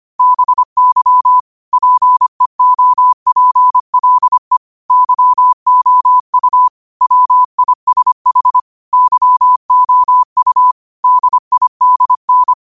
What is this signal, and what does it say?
Morse code, heard as a keyed tone, about 25 wpm, 1000 Hz. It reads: BY PEOPLE YOU WISH YOU DIDN